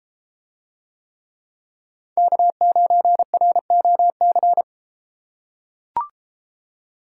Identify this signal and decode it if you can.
Morse code, heard as a keyed tone, about 30 wpm, 700 Hz. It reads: K9ROC